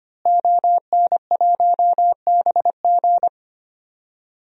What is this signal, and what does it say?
Morse code, heard as a keyed tone, about 25 wpm, 700 Hz. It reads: ON1BG
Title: DXCC-100 Call Signs - 5 Characters - Repeated 3x - 25wpm